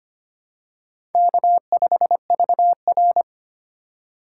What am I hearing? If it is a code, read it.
Morse code, heard as a keyed tone, about 25 wpm, 700 Hz. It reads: K5VR
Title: K1USN SST QSO Practice - 25wpm